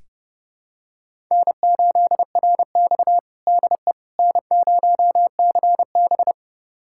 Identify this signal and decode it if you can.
Morse code, heard as a keyed tone, about 30 wpm, 700 Hz. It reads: N8RX DE N0CB
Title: USA Callsign DE Callsign - 4 Characters - 30wpm